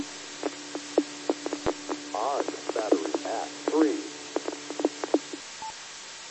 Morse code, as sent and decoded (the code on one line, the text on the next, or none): none